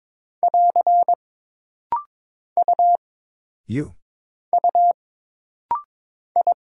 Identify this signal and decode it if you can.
Morse code, heard as a keyed tone, about 22 wpm, 700 Hz. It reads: <AR> U U I
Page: This rapid-fire timing is geared to push your ICR (Instant Character Recognition) skills.